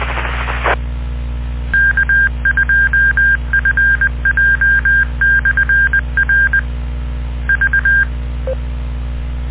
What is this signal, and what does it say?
Morse code, heard as a keyed tone, about 20 wpm, 1600 Hz. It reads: E K2FJ/R V